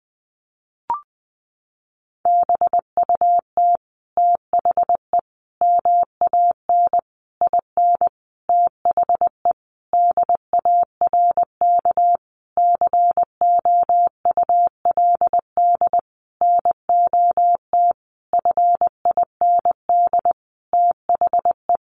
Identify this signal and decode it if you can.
Morse code, heard as a keyed tone, about 20 wpm, 700 Hz. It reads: BUT THE MAN IN THE DARK COULD NOT FIND THE
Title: Sentences from Top 600 Words 20wpm